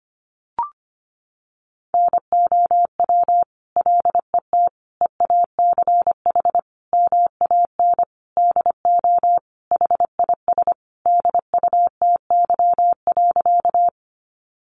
Morse code, heard as a keyed tone, about 25 wpm, 700 Hz. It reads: NOW LET EACH MAN DO HIS DUTY.
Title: Sentences from Top 1500 Words 25wpm